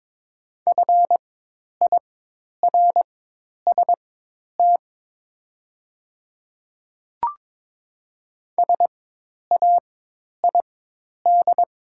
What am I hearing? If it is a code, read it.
Morse code, heard as a keyed tone, about 22 wpm, spaced as for 10 wpm, 700 Hz. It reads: FIRST SAID
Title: 4x Character Spacing - Top 100 Words - 22wpm